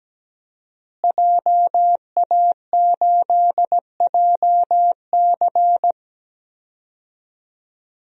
Morse code, heard as a keyed tone, about 17 wpm, 700 Hz. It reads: JA8JC
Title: DXCC-100 Call Signs - 5 Characters - Repeated 2x - 17wpm